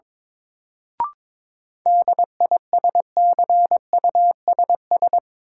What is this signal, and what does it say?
Morse code, heard as a keyed tone, about 22 wpm, 700 Hz. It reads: DISCUSS
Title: Sentences from Top 1000 Words - New Word Drill - 22wpm